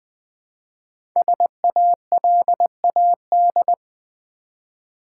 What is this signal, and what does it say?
Morse code, heard as a keyed tone, about 21 wpm, 700 Hz. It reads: SALAD
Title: Build A Word - 1 Letter At A Time - 3 to 5-Letter Words - 20wpm